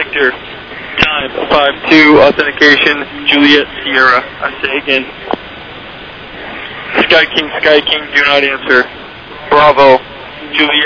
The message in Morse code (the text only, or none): none